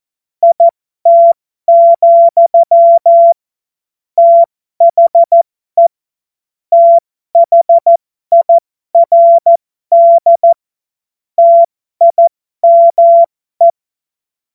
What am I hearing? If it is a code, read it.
Morse code, heard as a keyed tone, about 14 wpm, spaced as for 12 wpm, 680 Hz. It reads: IT, THE THIRD TIME
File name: Morsekurs / Witze